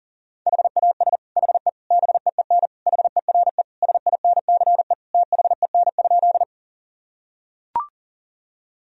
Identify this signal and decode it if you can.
Morse code, heard as a keyed tone, about 40 wpm, 700 Hz. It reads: HAS HE BEEN HERE SINCE THEN?
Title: Sentences from Top 1000 Words - Review Part 1 - 40wpm